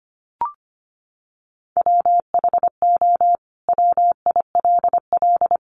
Morse code, heard as a keyed tone, about 25 wpm, 700 Hz. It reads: WHO WILL
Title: Sentences from Top 1100 Words 25wpm